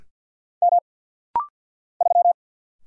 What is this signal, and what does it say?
Morse code, heard as a keyed tone, about 50 wpm, 700 Hz. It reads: M 3
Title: Single Letter-Number - Rapid-Fire 50wpm